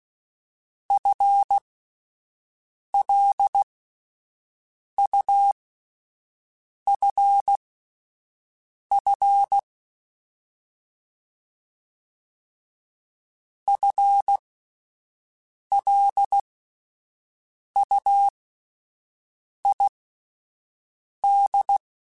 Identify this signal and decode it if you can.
Morse code, heard as a keyed tone, about 16 wpm, spaced as for 5 wpm, 800 Hz. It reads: FLUFF FLUID
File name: Spaced / FL Runs sp mp3s / FL Runs sp @10